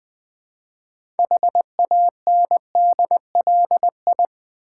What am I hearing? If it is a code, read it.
Morse code, heard as a keyed tone, about 20 wpm, 700 Hz. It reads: HANDLI
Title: Build A Word - 2 Letters At A Time - 8-Letter Words - 20wpm